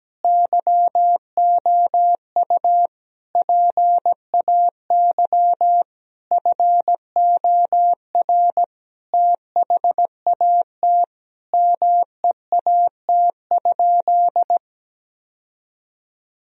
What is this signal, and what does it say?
Morse code, heard as a keyed tone, about 17 wpm, 700 Hz. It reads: YOU PAY FOR THAT MEAT?
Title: Sentences from Top 1000 Words - Review Part 3 - 17wpm